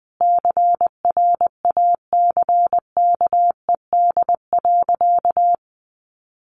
Morse code, heard as a keyed tone, about 20 wpm, 700 Hz. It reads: CRACKED.
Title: A Sound Of Thunder 20wpm